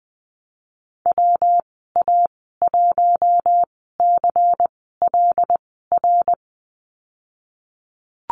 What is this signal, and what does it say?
Morse code, heard as a keyed tone, about 20 wpm, spaced as for 15 wpm, 700 Hz. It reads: WA1CLR
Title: 154 Lesson 25g - Letter p - Call Signs 20wpm